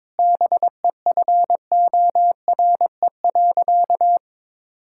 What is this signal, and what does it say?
Morse code, heard as a keyed tone, about 22 wpm, 700 Hz. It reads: BEFORE.
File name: practice / 22wpm / Sentences with 8-Word Phrases - 22wpm